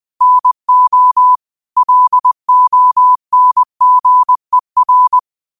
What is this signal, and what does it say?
Morse code, heard as a keyed tone, about 20 wpm, 1000 Hz. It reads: NO LONGER
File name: qotdfiles / 20wpm